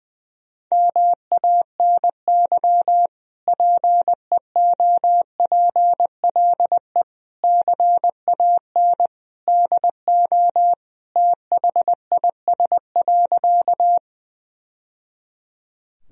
Morse code, heard as a keyed tone, about 21 wpm, 700 Hz. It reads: MANY PEOPLE CAN DO THIS.
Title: Sentences from Top 1000 Words - Review Part 1 - 20wpm